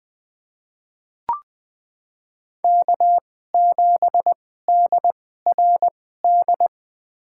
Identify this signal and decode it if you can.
Morse code, heard as a keyed tone, about 20 wpm, spaced as for 15 wpm, 700 Hz. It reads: K7DRD
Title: USA Call Signs - 5 Characters - 2x Character Spacing - 20wpm